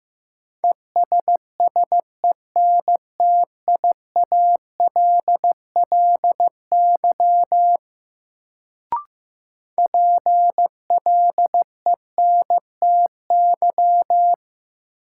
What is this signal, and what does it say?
Morse code, heard as a keyed tone, about 15 wpm, 700 Hz. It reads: ESSENTIALLY PLENTY
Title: Sentences from Top 1300 Words - New Word Drill Alt - 15wpm